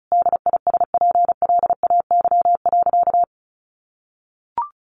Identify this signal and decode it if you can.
Morse code, heard as a keyed tone, about 34 wpm, 700 Hz. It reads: DISPLAY.
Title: Sentences from Top 1200 Words 35wpm